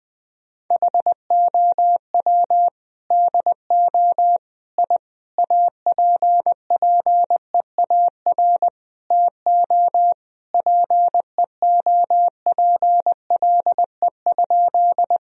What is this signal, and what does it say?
Morse code, heard as a keyed tone, about 20 wpm, 700 Hz. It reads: HOW DO I APPEAR TO PEOPLE?
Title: Sentences from Top 1000 Words - Review Part 2 - 20wpm